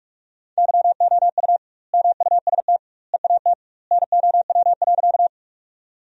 Morse code, heard as a keyed tone, about 47 wpm, 700 Hz. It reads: YOU MUST EAT NOW.
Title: 2x Word-Spacing - Sentences from Top 300 Words - 45wpm